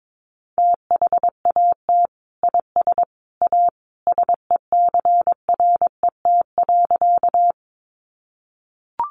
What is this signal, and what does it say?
Morse code, heard as a keyed tone, about 22 wpm, 700 Hz. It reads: THAT IS A SECRET.
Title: Sentences from Top 1200 Words 22wpm